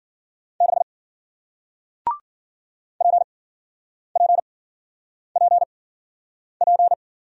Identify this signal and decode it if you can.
Morse code, heard as a keyed tone, about 52 wpm, spaced as for 9 wpm, 700 Hz. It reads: 6 PPPP